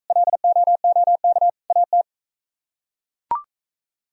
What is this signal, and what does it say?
Morse code, heard as a keyed tone, about 43 wpm, 700 Hz. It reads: LOOK AT